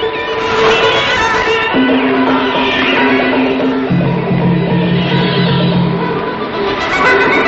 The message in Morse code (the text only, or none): none